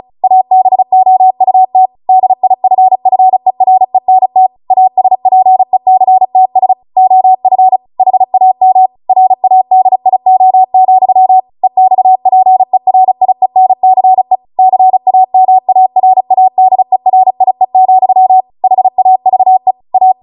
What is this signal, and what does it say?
Morse code, heard as a keyed tone, about 36 wpm, 750 Hz. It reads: ABOUT DIFFERENT ASPECTS OF HAM RADIO, EXPERIENCE CAMARADERIE, HAVE A